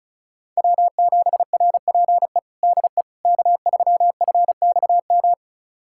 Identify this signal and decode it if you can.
Morse code, heard as a keyed tone, about 36 wpm, 700 Hz. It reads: W7RPE DE K3FXM
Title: USA Callsign DE Callsign - 5 Characters - 35wpm